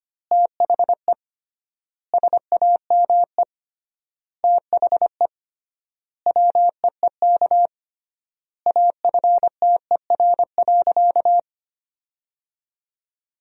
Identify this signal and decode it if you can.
Morse code, heard as a keyed tone, about 25 wpm, 700 Hz. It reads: THE SAME THE WEEK AFTER.
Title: 3x Word-Spacing - Sentences from Top 500 Words - 25wpm